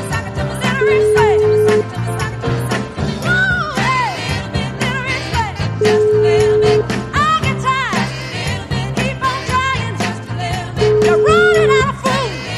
Ringback tone (Europe, 425 Hz), with noise.